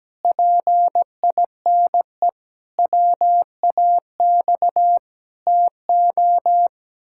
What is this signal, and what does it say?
Morse code, heard as a keyed tone, about 17 wpm, 700 Hz. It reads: PINE WAX TO
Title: Pangram Challenge v2 - 17wpm